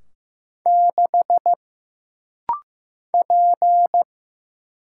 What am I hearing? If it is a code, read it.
Morse code, heard as a keyed tone, about 15 wpm, 700 Hz. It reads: 6 P
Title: Single Letter-Number Alt - Rapid-Fire 15wpm